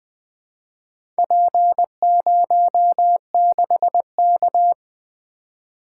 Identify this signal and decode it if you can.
Morse code, heard as a keyed tone, about 20 wpm, 700 Hz. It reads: P06K
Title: Four Letter-Number Combinations - 20wpm